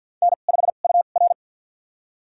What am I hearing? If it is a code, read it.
Morse code, heard as a keyed tone, about 47 wpm, spaced as for 35 wpm, 700 Hz. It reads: N5UR
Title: USA Call Signs - 4 Characters - 2x Character Spacing - 50wpm